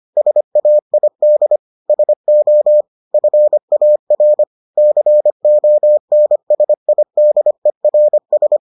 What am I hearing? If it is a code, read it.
Morse code, heard as a keyed tone, about 25 wpm, 600 Hz. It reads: SAID SO FAR CONSIDERS